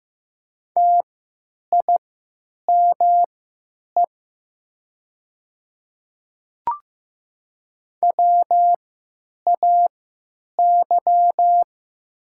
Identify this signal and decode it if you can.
Morse code, heard as a keyed tone, about 15 wpm, spaced as for 9 wpm, 700 Hz. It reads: TIME WAY
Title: 3x Character Spacing - Top 100 Words - 15wpm